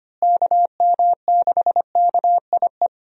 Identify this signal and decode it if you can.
Morse code, heard as a keyed tone, about 25 wpm, 700 Hz. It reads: KM6KIE
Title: USA Call Signs - 6 Characters - Repeated 3x - 25wpm